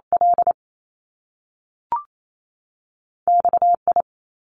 Morse code, heard as a keyed tone, about 28 wpm, 700 Hz. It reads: L XI
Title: Top Two Letter Combinations in Words 28wpm